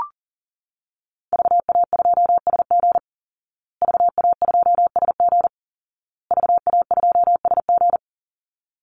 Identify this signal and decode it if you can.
Morse code, heard as a keyed tone, about 40 wpm, 700 Hz. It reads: VA2SG VA2SG VA2SG
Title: Call Signs Repeated 2x - 40wpm